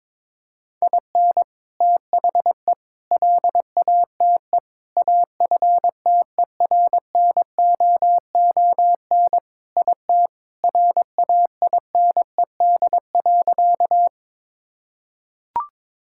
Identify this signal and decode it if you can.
Morse code, heard as a keyed tone, about 22 wpm, 700 Hz. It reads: IN THE LATE AFTERNOON IT RAINED.
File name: practice / 22wpm / Sentences with 3-Word Phrases - 22wpm